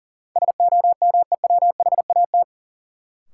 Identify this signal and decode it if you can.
Morse code, heard as a keyed tone, about 40 wpm, 700 Hz. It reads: SOMEWHAT
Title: Sentences from Top 1000 Words - Review Part 1 - New Word Drill - 40wpm